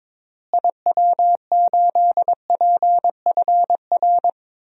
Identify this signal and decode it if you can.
Morse code, heard as a keyed tone, about 21 wpm, 700 Hz. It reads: IW8PFR
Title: DXCC-100 Call Signs - 6 Characters - Repeated 3x - 22wpm